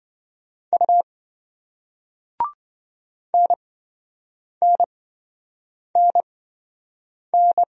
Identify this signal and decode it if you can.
Morse code, heard as a keyed tone, about 28 wpm, spaced as for 7 wpm, 700 Hz. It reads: U NNNN